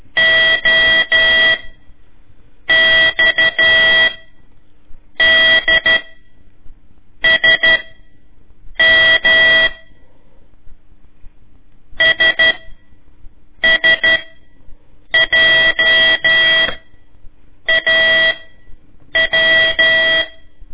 Morse code, heard as a keyed tone, about 12 wpm, spaced as for 6 wpm, 1950 Hz. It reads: OXDSM SSJAW
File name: ik1weg2 / CORSI CW / lez4-1